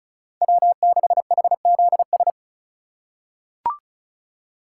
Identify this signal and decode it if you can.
Morse code, heard as a keyed tone, about 34 wpm, 700 Hz. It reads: W6HZS
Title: USA Call Signs - 5 Characters - 2x Character Spacing - 35wpm